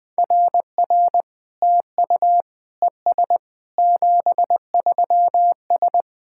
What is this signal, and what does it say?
Morse code, heard as a keyed tone, about 20 wpm, 700 Hz. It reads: RR TU ES 73S
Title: SOTA QSO Practice 20wpm